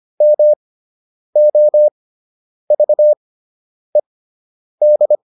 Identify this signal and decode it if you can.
Morse code, heard as a keyed tone, about 25 wpm, spaced as for 9 wpm, 600 Hz. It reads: MOVED